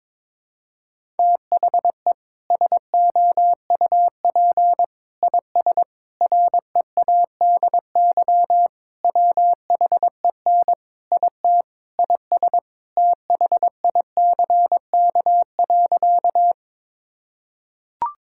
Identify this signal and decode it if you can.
Morse code, heard as a keyed tone, about 22 wpm, 700 Hz. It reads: THE SOUP IS READY WHEN IT IS THICK.
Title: Sentences from Top 1600 Words 22wpm